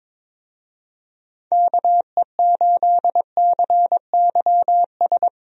Morse code, heard as a keyed tone, about 22 wpm, 700 Hz. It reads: KE8CYS
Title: USA Call Signs - 6 Characters - Repeated 2x - 22wpm